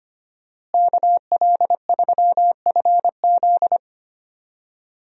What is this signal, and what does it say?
Morse code, heard as a keyed tone, about 25 wpm, 700 Hz. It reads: KL3FZ
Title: USA Call Signs - 5 Characters - Repeated 3x - 25wpm